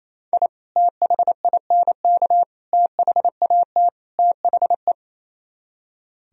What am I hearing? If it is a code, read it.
Morse code, heard as a keyed tone, about 36 wpm, 700 Hz. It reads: I THINK THAT THE